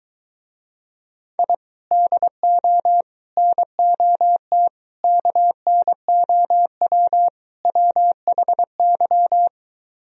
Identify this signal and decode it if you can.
Morse code, heard as a keyed tone, about 23 wpm, 700 Hz. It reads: I DO NOT KNOW WHY